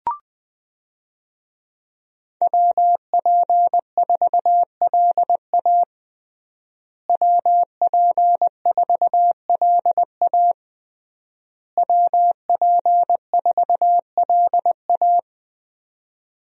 Morse code, heard as a keyed tone, about 20 wpm, 700 Hz. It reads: WP4LA WP4LA WP4LA
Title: USA Call Signs - 5 Characters - Repeated 3x - 20wpm